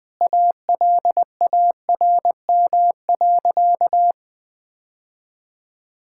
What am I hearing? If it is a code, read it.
Morse code, heard as a keyed tone, about 20 wpm, 700 Hz. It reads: ALARM.
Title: Sentences with 8-Word Phrases - 20wpm